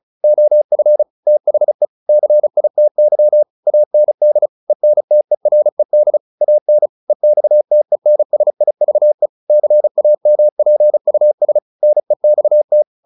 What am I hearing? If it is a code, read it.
Morse code, heard as a keyed tone, about 37 wpm, 600 Hz. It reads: OF THE CITY AND ENTERED AN EXTENSIVE CAMPUS NEXT